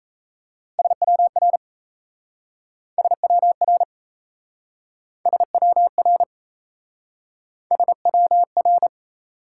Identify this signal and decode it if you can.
Morse code, heard as a keyed tone, about 37 wpm, 700 Hz. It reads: SWR SWR SWR SWR